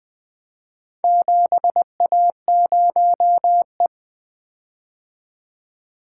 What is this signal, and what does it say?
Morse code, heard as a keyed tone, about 20 wpm, 700 Hz. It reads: 7A0E
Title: Four Letter-Number Combinations - 20wpm